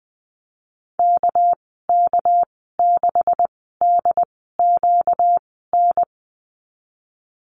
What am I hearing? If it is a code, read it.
Morse code, heard as a keyed tone, about 20 wpm, 700 Hz. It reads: K K 6 D Q N
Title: 238 Lesson 38g - Letter q - Call Signs 20wpm